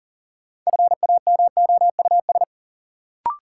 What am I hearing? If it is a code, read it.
Morse code, heard as a keyed tone, about 38 wpm, 700 Hz. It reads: FAMOUS
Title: Sentences from Top 1500 Words - New Word Drill Alt - 40wpm